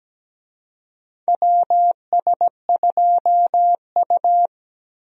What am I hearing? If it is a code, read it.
Morse code, heard as a keyed tone, about 17 wpm, 700 Hz. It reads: WS2U
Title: USA Call Signs - 4 Characters - Repeated 2x - 17wpm